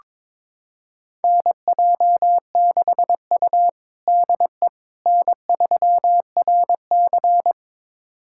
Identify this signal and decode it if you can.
Morse code, heard as a keyed tone, about 21 wpm, 700 Hz. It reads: NJ6U DE N3RC
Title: USA Callsign DE Callsign - 4 Characters - 22wpm